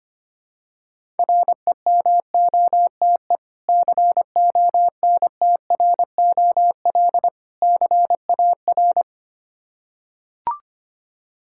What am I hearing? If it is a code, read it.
Morse code, heard as a keyed tone, about 25 wpm, 700 Hz. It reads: REMOTE CONTROL CAR
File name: practice / 25wpm / Difficult Words with Hints - 25wpm